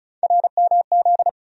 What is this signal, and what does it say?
Morse code, heard as a keyed tone, about 36 wpm, 700 Hz. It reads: RMZ
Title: USA Call Sign Postfixes - 3 Letters 35wpm